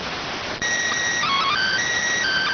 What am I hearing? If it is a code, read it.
SSTV (slow-scan television), PD 290.